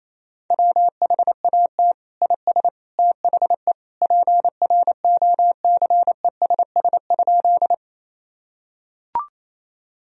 Morse code, heard as a keyed tone, about 28 wpm, 700 Hz. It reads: WHAT IS THE PROCESS?
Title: Sentences from Top 1000 Words - Review Part 3 - 28wpm